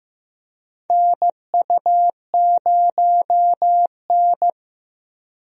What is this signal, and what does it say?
Morse code, heard as a keyed tone, about 15 wpm, 700 Hz. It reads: NU0N
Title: USA Call Signs - 4 Characters - Repeated 2x - 15wpm